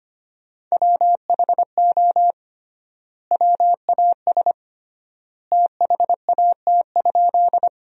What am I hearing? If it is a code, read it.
Morse code, heard as a keyed tone, about 25 wpm, 700 Hz. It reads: WHO WAS THAT?
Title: 3x Word-Spacing - Sentences from Top 100 Words - 25wpm